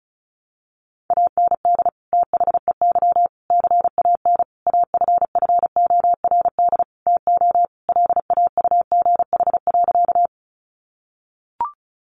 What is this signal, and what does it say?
Morse code, heard as a keyed tone, about 34 wpm, 700 Hz. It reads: AND THEY CAN AFFORD TO LAUGH.
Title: Sentences from Top 1100 Words 35wpm